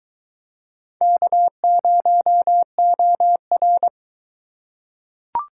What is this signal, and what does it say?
Morse code, heard as a keyed tone, about 22 wpm, 700 Hz. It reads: K0OR